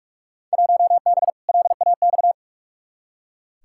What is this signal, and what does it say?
Morse code, heard as a keyed tone, about 44 wpm, 700 Hz. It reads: 1B LAX
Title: Field Day Exchanges - 45wpm